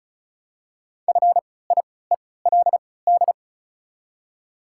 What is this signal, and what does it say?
Morse code, heard as a keyed tone, about 34 wpm, spaced as for 20 wpm, 700 Hz. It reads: FIELD
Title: 3x Character Spacing - Top 500 Words - 35wpm